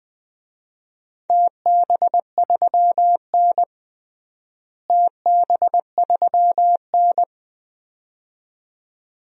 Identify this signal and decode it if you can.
Morse code, heard as a keyed tone, about 20 wpm, 700 Hz. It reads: TB3N TB3N
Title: DXCC-100 Call Signs - 4 Characters - Repeated 2x - 20wpm